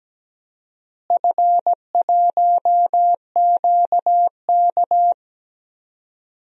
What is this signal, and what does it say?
Morse code, heard as a keyed tone, about 17 wpm, 700 Hz. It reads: F1QK
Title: DXCC-100 Call Signs - 4 Characters - Repeated 2x - 17wpm